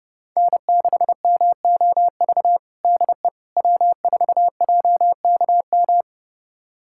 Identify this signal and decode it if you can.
Morse code, heard as a keyed tone, about 32 wpm, 700 Hz. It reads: N6MOV DE W4JKM
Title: USA Callsign DE Callsign - 5 Characters - 30wpm